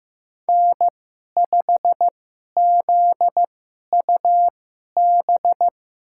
Morse code, heard as a keyed tone, about 15 wpm, spaced as for 11 wpm, 700 Hz. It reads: N5ZUB
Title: USA Call Signs - 5 Characters - 2x Character Spacing - 15wpm